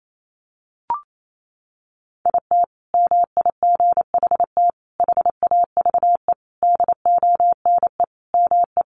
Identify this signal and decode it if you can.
Morse code, heard as a keyed tone, about 28 wpm, 700 Hz. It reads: IT MIGHT HAVE DONE ME
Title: Sentences from Top 500 Words 28wpm